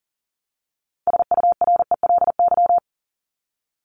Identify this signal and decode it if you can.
Morse code, heard as a keyed tone, about 40 wpm, 700 Hz. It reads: SURELY
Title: LY Suffix - Top 100 Words with Base Words 40wpm